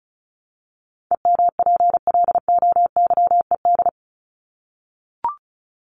Morse code, heard as a keyed tone, about 34 wpm, 700 Hz. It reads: EMPLOYED
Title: ED Suffix - Top 100 Words with Base Words 35wpm